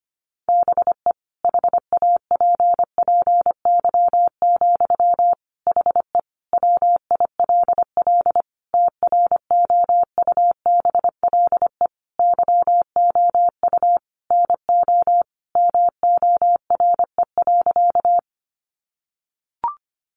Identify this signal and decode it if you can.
Morse code, heard as a keyed tone, about 25 wpm, 700 Hz. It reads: BE HAPPY, HE WILL TROUBLE YOU NO MORE.
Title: Sentences from Top 1200 Words 25wpm